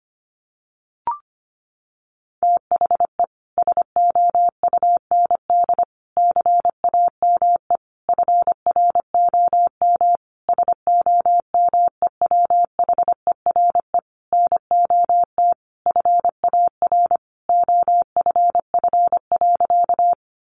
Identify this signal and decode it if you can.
Morse code, heard as a keyed tone, about 25 wpm, 700 Hz. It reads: THE SOUND CAME FROM SOMEWHERE NOT FAR OFF.
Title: Sentences from Top 1400 Words 25wpm